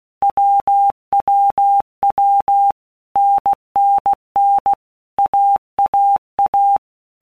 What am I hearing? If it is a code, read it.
Morse code, heard as a keyed tone, about 16 wpm, 800 Hz. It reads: WWW NNN AAA
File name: All Groups 1-minute Blocks / Groups@15 / AWUNDG